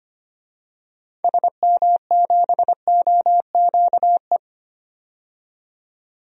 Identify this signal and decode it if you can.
Morse code, heard as a keyed tone, about 25 wpm, 700 Hz. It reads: SM7OQE
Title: DXCC-100 Call Signs - 6 Characters - 25wpm